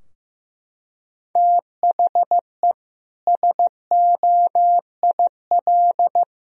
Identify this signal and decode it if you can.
Morse code, heard as a keyed tone, about 16 wpm, 700 Hz. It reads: THE SOIL
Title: Sentences from Top 1000 Words - Review Part 4 - 15wpm